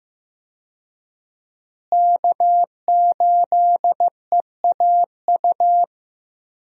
Morse code, heard as a keyed tone, about 15 wpm, 700 Hz. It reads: K8EAU
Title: USA Call Signs - 5 Characters - Repeated 2x - 15wpm